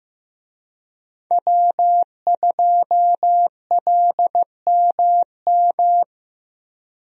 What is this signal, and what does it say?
Morse code, heard as a keyed tone, about 15 wpm, 700 Hz. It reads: W2LMM
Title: USA Call Signs - 5 Characters - Repeated 2x - 15wpm